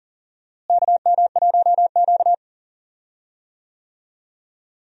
Morse code, heard as a keyed tone, about 40 wpm, 700 Hz. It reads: KM1Q
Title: USA Call Signs - 4 Characters - Repeated 3x - 40wpm